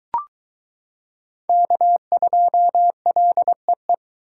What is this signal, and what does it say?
Morse code, heard as a keyed tone, about 23 wpm, 700 Hz. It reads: K2LEE